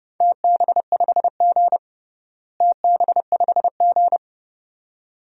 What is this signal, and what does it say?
Morse code, heard as a keyed tone, about 32 wpm, 700 Hz. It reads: TB5G TB5G
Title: DXCC-100 Call Signs - 4 Characters - Repeated 2x - 30wpm